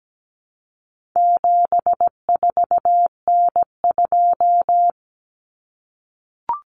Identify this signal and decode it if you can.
Morse code, heard as a keyed tone, about 17 wpm, 700 Hz. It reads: 74N2
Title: Common Cut Numbers - 4-Digit 17wpm